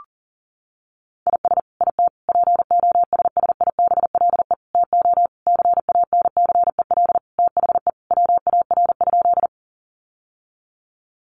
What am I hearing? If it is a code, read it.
Morse code, heard as a keyed tone, about 40 wpm, 700 Hz. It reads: IS IT POSSIBLE TO CANCEL THE WAR?
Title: Sentences from Top 1200 Words 40wpm